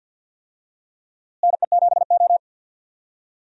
Morse code, heard as a keyed tone, about 46 wpm, 700 Hz. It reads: NE7O
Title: USA Call Signs - 4 Characters - Repeated 2x - 50wpm